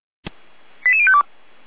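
Five-tone selective call (ZVEI), heard as quick runs of five tones, tones of about 70 ms. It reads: 88052
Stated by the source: ZVEI five-tone call received by SDR.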